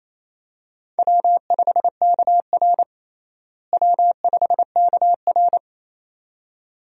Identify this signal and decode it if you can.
Morse code, heard as a keyed tone, about 28 wpm, 700 Hz. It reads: W5KR W5KR
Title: USA Call Signs - 4 Characters - Repeated 2x - 28wpm